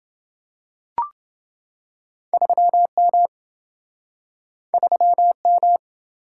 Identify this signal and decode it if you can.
Morse code, heard as a keyed tone, about 27 wpm, 700 Hz. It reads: 3M 3M